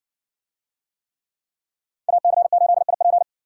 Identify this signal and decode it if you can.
Morse code, heard as a keyed tone, about 60 wpm, 700 Hz. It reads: AC8IP